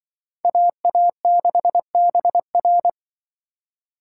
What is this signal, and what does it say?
Morse code, heard as a keyed tone, about 25 wpm, 700 Hz. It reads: AA6BR
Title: Speed Racing - USA 2x2 Call Signs 20wpm